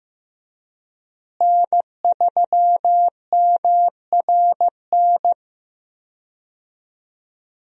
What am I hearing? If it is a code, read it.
Morse code, heard as a keyed tone, about 15 wpm, 700 Hz. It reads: N3MRN
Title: USA Call Signs - 5 Characters - Repeated 3x - 15wpm